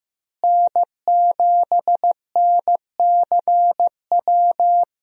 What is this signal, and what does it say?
Morse code, heard as a keyed tone, about 15 wpm, 700 Hz. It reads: N7NCW
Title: USA Call Signs - 5 Characters - Repeated 2x - 15wpm